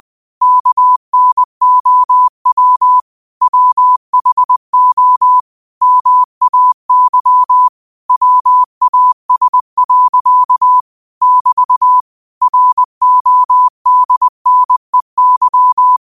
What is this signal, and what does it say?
Morse code, heard as a keyed tone, about 20 wpm, 1000 Hz. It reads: KNOW WHO MAY WAS. <BT> RODNEY
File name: qotdfiles / 20wpm